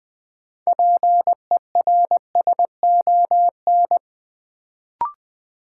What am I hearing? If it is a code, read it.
Morse code, heard as a keyed tone, about 20 wpm, 700 Hz. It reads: PERSON
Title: Sentences from Top 400 Words - New Word Drill - 20wpm